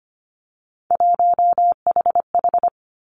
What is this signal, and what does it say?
Morse code, heard as a keyed tone, about 25 wpm, 700 Hz. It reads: 1HH
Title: Three Challenging Letter-Number Combinations 25wpm